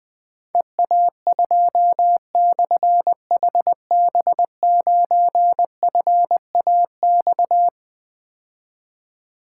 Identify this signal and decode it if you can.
Morse code, heard as a keyed tone, about 21 wpm, 700 Hz. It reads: EA2/HB9FAX
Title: Difficult Call Signs 20wpm